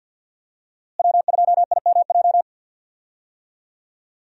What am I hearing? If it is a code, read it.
Morse code, heard as a keyed tone, about 51 wpm, 700 Hz. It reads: W2IMJ
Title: USA Call Signs - 5 Characters - Repeated 3x - 50wpm